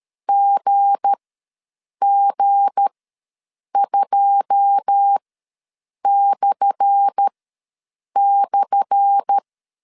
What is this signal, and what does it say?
Morse code, heard as a keyed tone, about 13 wpm, spaced as for 7 wpm, 800 Hz. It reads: GG2//
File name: LEZIONI / 10 SIGNS / 10C DOT ws 12wpm 3548